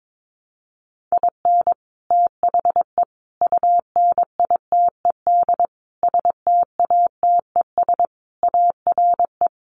Morse code, heard as a keyed tone, about 22 wpm, 700 Hz. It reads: IN THE UNITED STATES ARE
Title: Sets of 5 Words Encore 22wpm